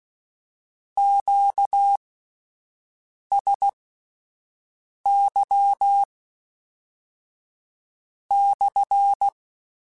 Morse code, heard as a keyed tone, about 16 wpm, spaced as for 5 wpm, 800 Hz. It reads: QSY /
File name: Q Calls / Q Calls-mp3s / Q Calls SPSL2 @10